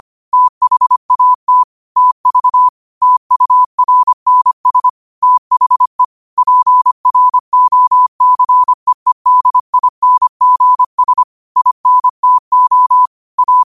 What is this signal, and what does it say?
Morse code, heard as a keyed tone, about 25 wpm, 1000 Hz. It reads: THAT TV TURNS THE PROCEEDINGS INTO A
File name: qotdfiles / 25wpm